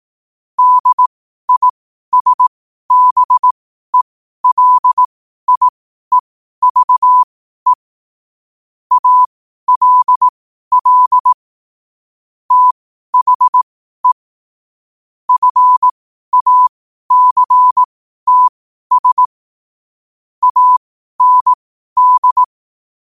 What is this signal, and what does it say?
Morse code, heard as a keyed tone, about 18 wpm, spaced as for 12 wpm, 1000 Hz. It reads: DISBELIEVE ALL THE FACTS AND
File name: qotdfiles / 12wpm